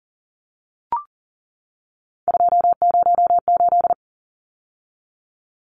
Morse code, heard as a keyed tone, about 40 wpm, 700 Hz. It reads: 208
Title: Numbers - 3-Digit 40wpm